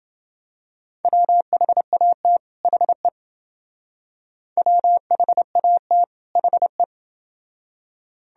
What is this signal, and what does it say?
Morse code, heard as a keyed tone, about 29 wpm, 700 Hz. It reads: WHAT HE WHAT HE